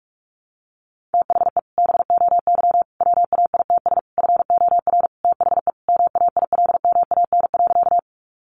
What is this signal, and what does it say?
Morse code, heard as a keyed tone, about 44 wpm, 700 Hz. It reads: THE BOY WAITS FOR THE MAILMAN.
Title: Easy Phrases 45wpm